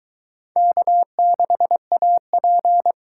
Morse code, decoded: K6AP

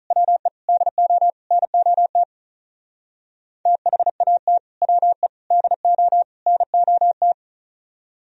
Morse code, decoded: WE DO NOT THAT WE DO NOT

